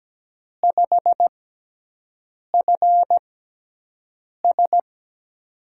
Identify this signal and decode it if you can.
Morse code, heard as a keyed tone, about 17 wpm, spaced as for 6 wpm, 700 Hz. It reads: 5FS
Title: USA Call Signs - 5 Characters - 6x Character Spacing - 17wpm